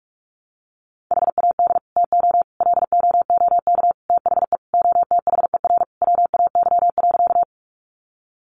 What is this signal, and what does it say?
Morse code, heard as a keyed tone, about 44 wpm, 700 Hz. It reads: HAD TO LOOK THE OTHER WAY.
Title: Sentences from Top 100 Words 45wpm